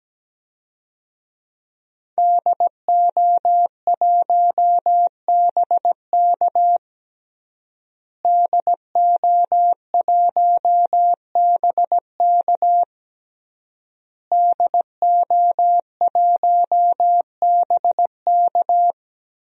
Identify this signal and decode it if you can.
Morse code, heard as a keyed tone, about 17 wpm, 700 Hz. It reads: DO1BK DO1BK DO1BK
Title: DXCC-100 Call Signs - 5 Characters - Repeated 3x - 17wpm